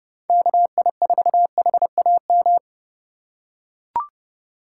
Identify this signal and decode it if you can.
Morse code, heard as a keyed tone, about 29 wpm, 700 Hz. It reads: KI4HAM
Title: USA Call Signs - 6 Characters - 6x Character Spacing - 30wpm